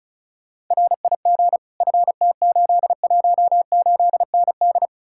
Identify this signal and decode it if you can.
Morse code, heard as a keyed tone, about 47 wpm, spaced as for 41 wpm, 700 Hz. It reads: RIG FT818ND
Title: Speed-Racing Common QSO Phrases - 35wpm